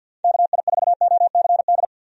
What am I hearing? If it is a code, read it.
Morse code, heard as a keyed tone, about 51 wpm, 700 Hz. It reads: KI4OCD
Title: USA Call Signs - 6 Characters - 50wpm